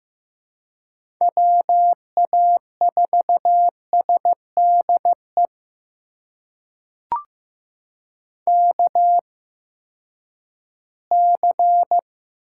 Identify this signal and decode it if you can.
Morse code, heard as a keyed tone, about 15 wpm, spaced as for 10 wpm, 700 Hz. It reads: WA4SDE KC
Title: USA Call Signs - 6 Characters - 8x Character Spacing - 15wpm